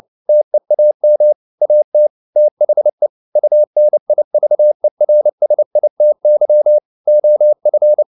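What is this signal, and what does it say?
Morse code, heard as a keyed tone, about 29 wpm, 600 Hz. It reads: TEAM AT THE UNIVERSITY OF
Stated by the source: Morse code at 29 wpm, sending Ken Follett's novel Never in English, from CW med Gnister.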